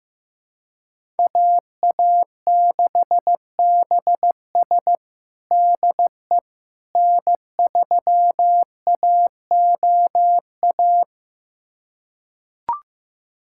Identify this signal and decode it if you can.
Morse code, heard as a keyed tone, about 15 wpm, 700 Hz. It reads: AA6BS DE N3AOA
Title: USA Callsign DE Callsign - 5 Characters - 15wpm